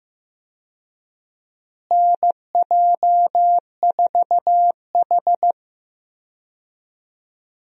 Morse code, heard as a keyed tone, about 15 wpm, 700 Hz. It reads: NJ4H
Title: USA Call Signs - 4 Characters - Repeated 2x - 15wpm